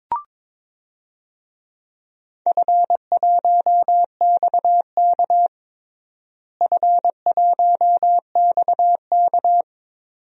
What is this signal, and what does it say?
Morse code, heard as a keyed tone, about 22 wpm, 700 Hz. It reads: F1XK F1XK
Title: DXCC-100 Call Signs - 4 Characters - Repeated 2x - 22wpm